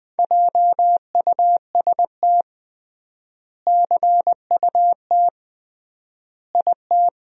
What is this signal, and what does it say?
Morse code, heard as a keyed tone, about 20 wpm, 700 Hz. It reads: JUST CUT IT
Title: 3x Word-Spacing - Sentences from Top 300 Words - 20wpm